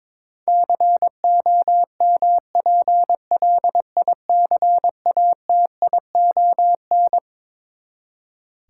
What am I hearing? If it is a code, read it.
Morse code, heard as a keyed tone, about 22 wpm, 700 Hz. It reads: COMPLICATION
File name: Twelve Letter Words - 22wpm